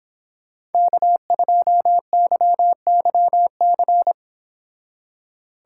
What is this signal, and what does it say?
Morse code, heard as a keyed tone, about 31 wpm, 700 Hz. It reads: K2YYC